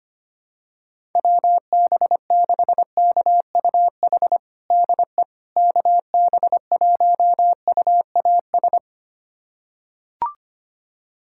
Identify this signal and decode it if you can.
Morse code, heard as a keyed tone, about 25 wpm, 700 Hz. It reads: WB6KUH DE KB1UAS
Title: USA Callsign DE Callsign - 6 Characters - 25wpm